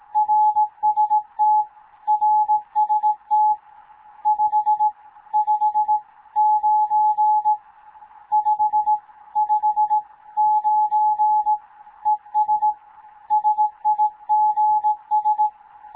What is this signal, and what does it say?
Morse code, heard as a keyed tone, about 18 wpm, 850 Hz. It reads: RST RST 5 5 9 5 5 9 ES SIGS
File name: Morze-HA2MN / hangp / 2 11 GYAKORLAT